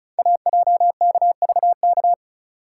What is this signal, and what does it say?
Morse code, heard as a keyed tone, about 35 wpm, 700 Hz. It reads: AJKVK
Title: Code Groups - 5 Letters - 35wpm